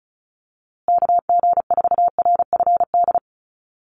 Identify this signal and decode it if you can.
Morse code, heard as a keyed tone, about 34 wpm, 700 Hz. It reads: KG4RFD
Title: USA Call Signs Repeated 3x 35wpm